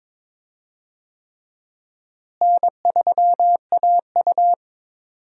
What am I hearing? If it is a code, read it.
Morse code, heard as a keyed tone, about 22 wpm, 700 Hz. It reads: N3AU
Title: USA Call Signs - 4 Characters - Repeated 3x - 22wpm